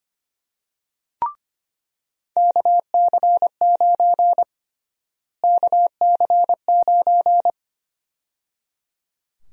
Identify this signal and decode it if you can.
Morse code, heard as a keyed tone, about 25 wpm, 700 Hz. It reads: KC9 KC9
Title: USA Call Sign Prefixes - Repeated 2x - 25wpm